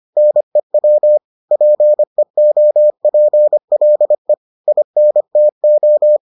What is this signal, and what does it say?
Morse code, heard as a keyed tone, about 25 wpm, 600 Hz. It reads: NEW PEOPLE INTO